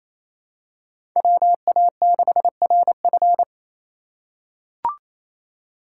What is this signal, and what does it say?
Morse code, heard as a keyed tone, about 27 wpm, 700 Hz. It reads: WA6RF
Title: USA Call Signs - 5 Characters - 4x Character Spacing - 28wpm